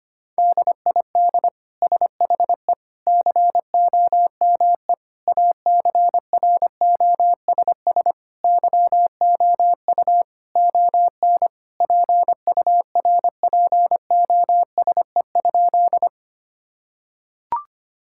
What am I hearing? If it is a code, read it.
Morse code, heard as a keyed tone, about 25 wpm, 700 Hz. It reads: DID SHE COME ACROSS YOU ON PURPOSE?
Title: Sentences from Top 1000 Words - Review Part 5 - 25wpm